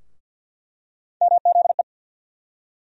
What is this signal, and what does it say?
Morse code, heard as a keyed tone, about 49 wpm, 700 Hz. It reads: MZE
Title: USA Call Sign Postfixes - 3 Letters 50wpm